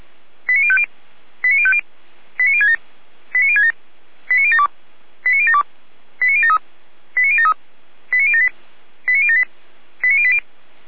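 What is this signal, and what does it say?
Five-tone selective call (ZVEI), heard as quick runs of five tones, tones of about 70 ms. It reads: 89050 89050 89067 89067 89072 89072 89073 89073 89078 89078 89080
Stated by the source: Five-tone ZVEI selective call received by SDR.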